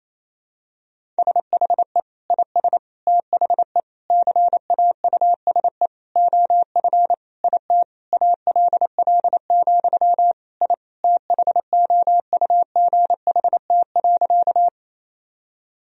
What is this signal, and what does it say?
Morse code, heard as a keyed tone, about 28 wpm, 700 Hz. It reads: SHE IS THE CAUSE OF IT ALL, I THOUGHT.
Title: Sentences from Top 1000 Words - Review Part 1 - 28wpm